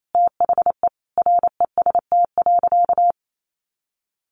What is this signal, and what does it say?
Morse code, heard as a keyed tone, about 28 wpm, 700 Hz. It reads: THE REST.
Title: Sentences from Top 1500 Words 28wpm